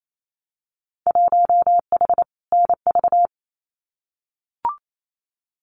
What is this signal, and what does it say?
Morse code, heard as a keyed tone, about 28 wpm, 700 Hz. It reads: 1H NV
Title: Winter Field Day Exchanges 28wpm